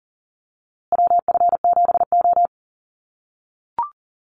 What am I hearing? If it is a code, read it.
Morse code, heard as a keyed tone, about 40 wpm, 700 Hz. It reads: WF7O
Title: USA Call Signs Repeated 2x 40wpm